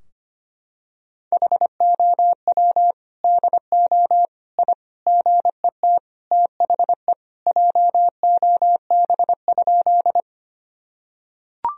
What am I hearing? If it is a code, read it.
Morse code, heard as a keyed tone, about 25 wpm, 700 Hz. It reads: HOW DO I GET THE JOB?
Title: Sentences from Top 1000 Words - Review Part 3 - 25wpm